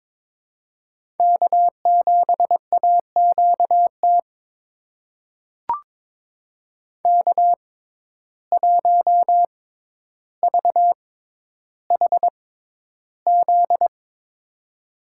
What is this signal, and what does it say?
Morse code, heard as a keyed tone, about 22 wpm, spaced as for 11 wpm, 700 Hz. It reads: K7AQT K1VHZ
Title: USA Call Signs - 5 Characters - 6x Character Spacing - 22wpm